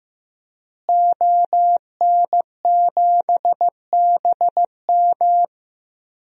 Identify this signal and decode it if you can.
Morse code, heard as a keyed tone, about 15 wpm, 700 Hz. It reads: ON7BM
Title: DXCC-100 Call Signs - 5 Characters - Repeated 3x - 15wpm